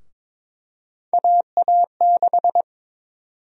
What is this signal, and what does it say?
Morse code, heard as a keyed tone, about 22 wpm, 700 Hz. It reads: AA6
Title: USA Call Sign Prefixes - 22wpm